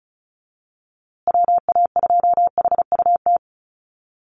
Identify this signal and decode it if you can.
Morse code, heard as a keyed tone, about 34 wpm, 700 Hz. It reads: WA2HUT
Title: USA Call Signs Repeated 2x 35wpm